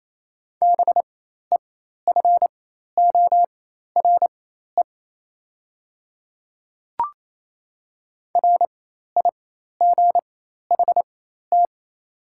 Morse code, heard as a keyed tone, about 28 wpm, spaced as for 13 wpm, 700 Hz. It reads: BEFORE RIGHT
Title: 4x Character Spacing - Top 200 Words - 28wpm